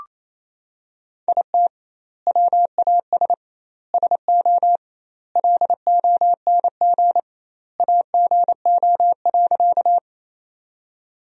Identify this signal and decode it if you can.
Morse code, heard as a keyed tone, about 28 wpm, 700 Hz. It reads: IT WAS SO LONG AGO.
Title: 2x Word-Spacing - Sentences from Top 500 Words - 28wpm